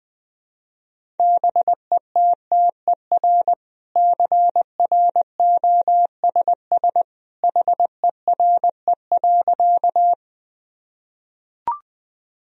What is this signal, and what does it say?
Morse code, heard as a keyed tone, about 20 wpm, 700 Hz. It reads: BETTER CROSS HERE.
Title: Sentences from Top 1000 Words - Review Part 1 - 20wpm